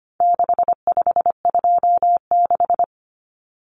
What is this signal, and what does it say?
Morse code, heard as a keyed tone, about 25 wpm, 700 Hz. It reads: 6526
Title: Numbers - 4-Digit 25wpm